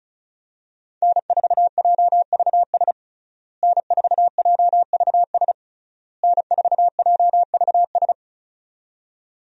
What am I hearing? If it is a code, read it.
Morse code, heard as a keyed tone, about 36 wpm, 700 Hz. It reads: N4JVS N4JVS N4JVS
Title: USA Call Signs - 5 Characters - Repeated 3x - 35wpm